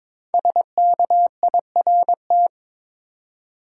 Morse code, heard as a keyed tone, about 22 wpm, 700 Hz. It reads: SKIRT
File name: 5 Letter Words - First 3 Letters Given - 22wpm